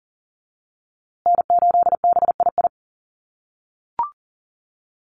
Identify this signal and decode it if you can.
Morse code, heard as a keyed tone, about 40 wpm, 700 Hz. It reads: N8BII
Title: USA Call Signs Repeated 2x 40wpm